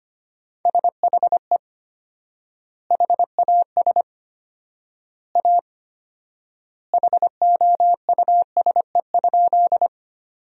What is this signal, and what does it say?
Morse code, heard as a keyed tone, about 25 wpm, 700 Hz. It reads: SHE HAS A HOUSE?
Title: 4x Word-Spacing - Sentences from Top 200 Words - 25wpm